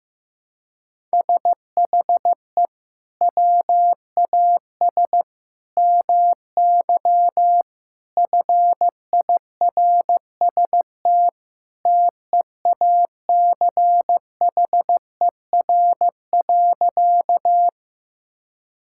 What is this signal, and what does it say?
Morse code, heard as a keyed tone, about 15 wpm, 700 Hz. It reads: SHE WAS MY FIRST TEACHER.
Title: Sentences from Top 1000 Words - Review Part 6 - 15wpm